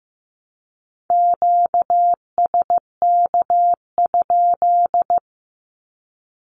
Morse code, heard as a keyed tone, about 15 wpm, 700 Hz. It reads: QSK?
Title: Q-Codes 15wpm